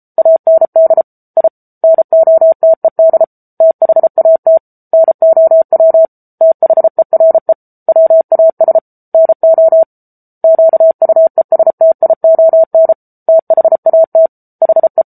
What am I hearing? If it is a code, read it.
Morse code, heard as a keyed tone, about 34 wpm, 650 Hz. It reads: AND I NOTED THAT NOW THERE WAS NO QUESTION THAT HE
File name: Book Two / 07-Chapter 7 - 33 WPM